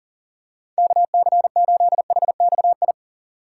Morse code, heard as a keyed tone, about 40 wpm, 700 Hz. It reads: KC8HXI
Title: USA Call Signs - 6 Characters - 6x Character Spacing - 40wpm